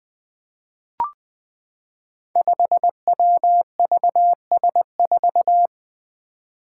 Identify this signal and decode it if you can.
Morse code, heard as a keyed tone, about 20 wpm, 700 Hz. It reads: 5WVS4
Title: Five Challenging Letter-Number Combinations 20wpm